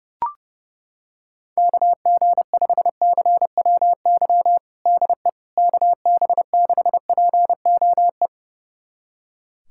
Morse code, heard as a keyed tone, about 29 wpm, 700 Hz. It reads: KG5CWY DE KB6POE
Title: USA Callsign DE Callsign - 6 Characters - 30wpm